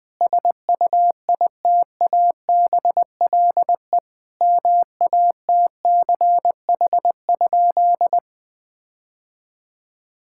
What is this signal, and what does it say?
Morse code, heard as a keyed tone, about 21 wpm, 700 Hz. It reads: SUITABLE MATCH?
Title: Sentences from Top 1600 Words 20wpm